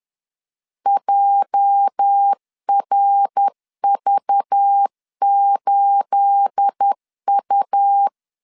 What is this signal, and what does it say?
Morse code, heard as a keyed tone, about 10 wpm, 800 Hz. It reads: JRV8U
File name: CW / DET / 1 9 1500C S B 10wpm 4439